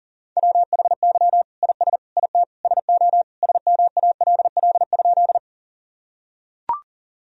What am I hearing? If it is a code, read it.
Morse code, heard as a keyed tone, about 37 wpm, 700 Hz. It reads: WHY IS IT SO SMALL?
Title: Sentences from Top 1000 Words - Review Part 1 - 40wpm